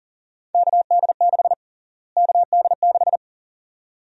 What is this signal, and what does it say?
Morse code, heard as a keyed tone, about 37 wpm, 700 Hz. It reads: KD6 KD6
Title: USA Call Sign Prefixes - Repeated 2x - 40wpm